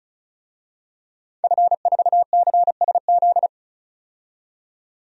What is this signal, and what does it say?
Morse code, heard as a keyed tone, about 36 wpm, 700 Hz. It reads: F4CSZ
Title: DXCC-100 Call Signs - 5 Characters - 35wpm